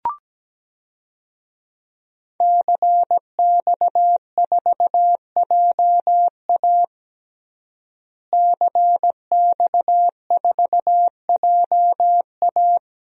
Morse code, heard as a keyed tone, about 17 wpm, 700 Hz. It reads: CX4JA CX4JA
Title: DXCC-100 Call Signs - 5 Characters - Repeated 2x - 17wpm